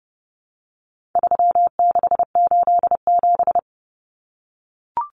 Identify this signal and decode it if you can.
Morse code, heard as a keyed tone, about 30 wpm, 700 Hz. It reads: 3687
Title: Numbers - 4-Digit 30wpm